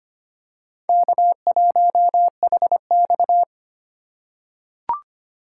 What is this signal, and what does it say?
Morse code, heard as a keyed tone, about 25 wpm, 700 Hz. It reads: K1HX
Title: USA Call Signs - 4 Characters - 8x Character Spacing - 25wpm